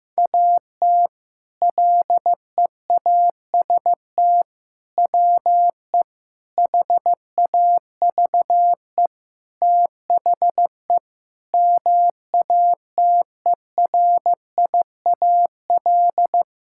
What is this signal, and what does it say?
Morse code, heard as a keyed tone, about 15 wpm, 700 Hz. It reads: AT LEAST WE HAVE THE MATERIAL
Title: Sentences from Top 1000 Words - Review Part 5 - 15wpm